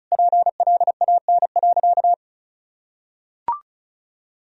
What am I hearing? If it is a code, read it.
Morse code, heard as a keyed tone, about 36 wpm, 700 Hz. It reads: PLAN.
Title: Sentences from Top 1000 Words - Review Part 2 - 35wpm